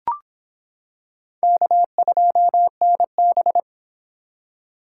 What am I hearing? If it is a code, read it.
Morse code, heard as a keyed tone, about 25 wpm, 700 Hz. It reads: K2NB